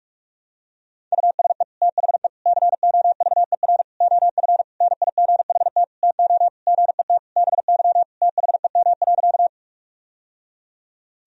Morse code, heard as a keyed tone, about 46 wpm, 700 Hz. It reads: USE THE COVER OF NIGHT TO GET BY THEM.
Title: Sentences from Top 1000 Words - Review Part 1 - 45wpm